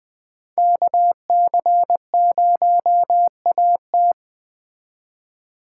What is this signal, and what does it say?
Morse code, heard as a keyed tone, about 20 wpm, 700 Hz. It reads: KC0AT
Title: Speed Racing - USA 2x2 Call Signs 17wpm